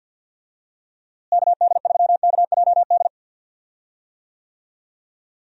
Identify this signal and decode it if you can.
Morse code, heard as a keyed tone, about 51 wpm, 700 Hz. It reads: KD3KJD
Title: USA Call Signs - 6 Characters - Repeated 2x - 50wpm